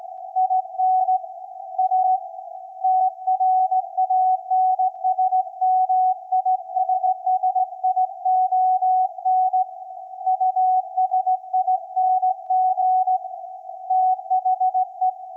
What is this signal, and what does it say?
Morse code, heard as a keyed tone, about 18 wpm, 740 Hz. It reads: IN A TRANSMISSION USING THE
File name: iz7slz / VLF / SAQ 24-12-2024